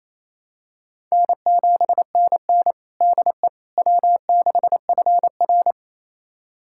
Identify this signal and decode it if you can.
Morse code, heard as a keyed tone, about 28 wpm, 700 Hz. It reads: N7NN DE W6FR
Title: USA Callsign DE Callsign - 4 Characters - 28wpm